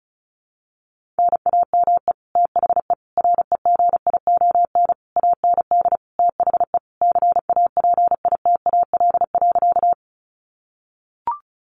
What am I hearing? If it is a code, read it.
Morse code, heard as a keyed tone, about 34 wpm, 700 Hz. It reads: NAME THE REGION AND THE CAPITAL.
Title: Sentences from Top 1500 Words 35wpm